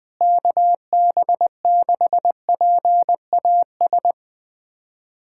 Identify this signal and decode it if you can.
Morse code, heard as a keyed tone, about 20 wpm, 700 Hz. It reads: KB6PAS
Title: USA Callsign DE Callsign 20wpm